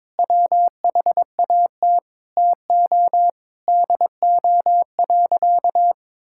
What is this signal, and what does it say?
Morse code, heard as a keyed tone, about 22 wpm, 700 Hz. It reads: WHAT TO DO.
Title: Sentences from Top 1000 Words - Review Part 5 - 22wpm